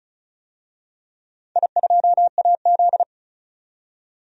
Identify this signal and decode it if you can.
Morse code, heard as a keyed tone, about 36 wpm, 700 Hz. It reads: I2AZ
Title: DXCC-100 Call Signs - 4 Characters - 35wpm